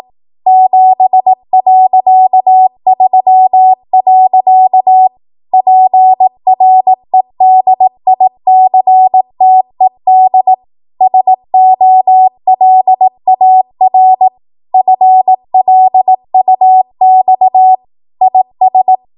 Morse code, heard as a keyed tone, about 18 wpm, 750 Hz. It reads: 7.3. PREDICTED SOLAR FLUX IS
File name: Archive / 18 WPM / 18WPM